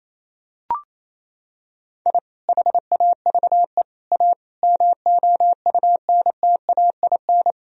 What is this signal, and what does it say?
Morse code, heard as a keyed tone, about 28 wpm, 700 Hz. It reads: I HAVE A MOUNTAIN
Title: Sentences from Top 1000 Words - Review Part 1 - 28wpm